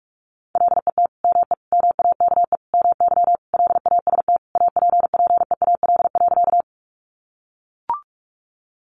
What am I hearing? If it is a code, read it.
Morse code, heard as a keyed tone, about 45 wpm, 700 Hz. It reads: LET ME MAKE MY LAST APPEAL.
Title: Sentences from Top 1300 Words 45wpm